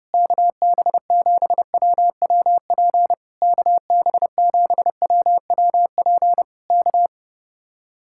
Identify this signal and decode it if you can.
Morse code, heard as a keyed tone, about 29 wpm, 700 Hz. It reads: KB7WWP KB7WWP K